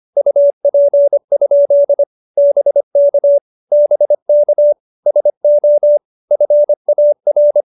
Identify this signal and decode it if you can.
Morse code, heard as a keyed tone, about 25 wpm, 570 Hz. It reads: UP? BK BK SO FAR